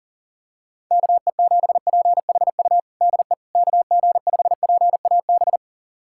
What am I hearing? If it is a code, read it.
Morse code, heard as a keyed tone, about 40 wpm, 700 Hz. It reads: KE7PHU DE KG5PAB
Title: USA Callsign DE Callsign - 6 Characters - 40wpm